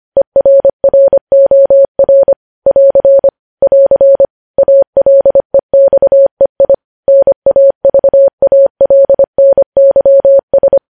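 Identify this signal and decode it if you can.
Morse code, heard as a keyed tone, about 25 wpm, 550 Hz. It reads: ERROR <AR> <AR> ALEXEI NAVALNYS
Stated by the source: Morse code news headlines at 25 wpm.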